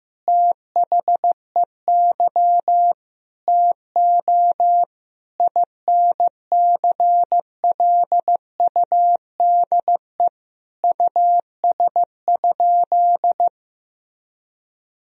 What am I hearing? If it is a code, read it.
Morse code, heard as a keyed tone, about 15 wpm, 700 Hz. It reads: THEY TO INCLUDE US?
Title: Sentences from Top 1000 Words - Review Part 3 - 15wpm